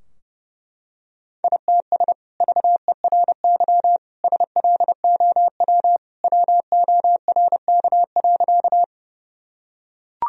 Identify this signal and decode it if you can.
Morse code, heard as a keyed tone, about 30 wpm, 700 Hz. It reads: ITS VERY SLOW WORK.
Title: Sentences from Top 1000 Words - Review Part 2 - 30wpm